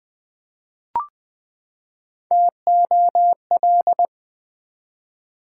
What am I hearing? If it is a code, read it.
Morse code, heard as a keyed tone, about 21 wpm, 700 Hz. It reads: TOL
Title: Build A Word - 3 Letters At A Time - 9-Letter Words - 20wpm